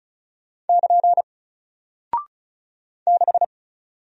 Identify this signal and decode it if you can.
Morse code, heard as a keyed tone, about 36 wpm, 700 Hz. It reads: <KN> 6
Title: Single Character (Letter:Number:Punctuation:Prosign) Alt 35wpm